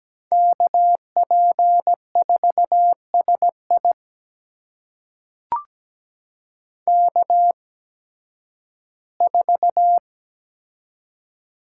Morse code, heard as a keyed tone, about 17 wpm, spaced as for 11 wpm, 700 Hz. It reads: KP4SI K4
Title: USA Call Signs - 8x Character Spacing - 17wpm